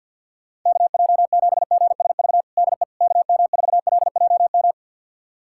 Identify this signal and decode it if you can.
Morse code, heard as a keyed tone, about 51 wpm, 700 Hz. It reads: KJ7GSV DE KM4LJM
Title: USA Callsign DE Callsign - 6 Characters - 50wpm